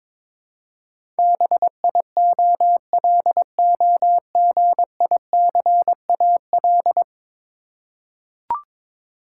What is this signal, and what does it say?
Morse code, heard as a keyed tone, about 22 wpm, 700 Hz. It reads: BIOLOGICAL
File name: practice / 22wpm / 10 Letter Words - First 3 Letters Given - 22wpm